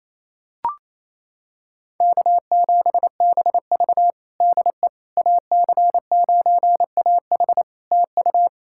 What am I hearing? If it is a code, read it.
Morse code, heard as a keyed tone, about 28 wpm, 700 Hz. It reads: K7BV DE AC9AH TU
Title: Practice QSO - Exchange 4 - Ending - 28wpm